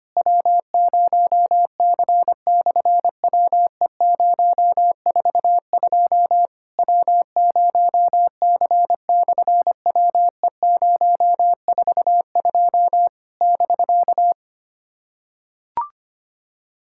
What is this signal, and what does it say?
Morse code, heard as a keyed tone, about 25 wpm, 700 Hz. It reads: W0C/WE042 W0C/WE042 <BK>
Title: SOTA QSO Practice 25wpm